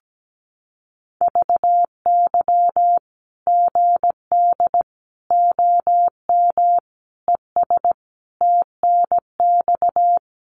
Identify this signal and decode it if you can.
Morse code, heard as a keyed tone, about 17 wpm, 700 Hz. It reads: VY GD OM ES TNX
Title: Common QSO Phrases 17wpm